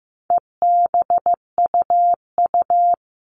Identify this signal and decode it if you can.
Morse code, heard as a keyed tone, about 15 wpm, 700 Hz. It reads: EBUU
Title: All Cut Numbers - 4-Digit 15wpm